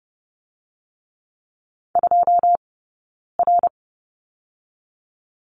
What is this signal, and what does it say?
Morse code, heard as a keyed tone, about 30 wpm, 700 Hz. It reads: 2 R
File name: 092 Lesson 16d - Letter c - Two Characters 30wpm